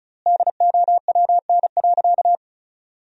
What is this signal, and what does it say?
Morse code, heard as a keyed tone, about 36 wpm, 700 Hz. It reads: DOWN.
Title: 3x Word-Spacing - Sentences from Top 500 Words - 35wpm